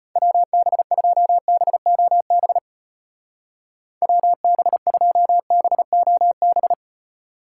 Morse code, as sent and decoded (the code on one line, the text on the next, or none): .-- -... ..--- -... --- -... / .-- -... ..--- -... --- -...
WB2BOB WB2BOB